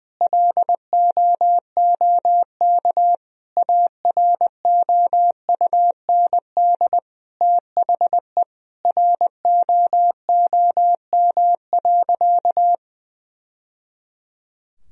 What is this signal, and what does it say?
Morse code, heard as a keyed tone, about 20 wpm, 700 Hz. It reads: LOOK AROUND THE ROOM.
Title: Sentences from Top 1000 Words - Review Part 3 - 20wpm